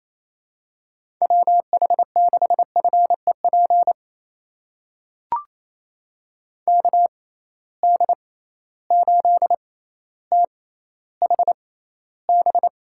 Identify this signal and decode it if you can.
Morse code, heard as a keyed tone, about 28 wpm, spaced as for 14 wpm, 700 Hz. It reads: WH6FEP KD8THB
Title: USA Call Signs - 6 Characters - 6x Character Spacing - 28wpm